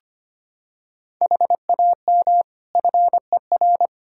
Morse code, heard as a keyed tone, about 25 wpm, 700 Hz. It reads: HAM FER
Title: Practice QSO - Exchange 3 - More Info - 25wpm